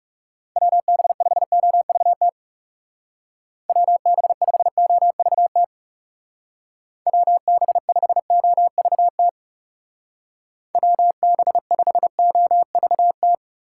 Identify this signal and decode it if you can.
Morse code, heard as a keyed tone, about 34 wpm, 700 Hz. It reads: WB5OVT WB5OVT WB5OVT WB5OVT
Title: Speed Racing - USA Call Signs 30wpm